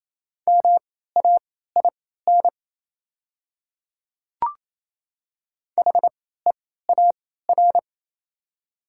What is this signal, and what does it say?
Morse code, heard as a keyed tone, about 28 wpm, spaced as for 16 wpm, 700 Hz. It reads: MAIN HEAR
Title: 3x Character Spacing - Top 300 Words - 28wpm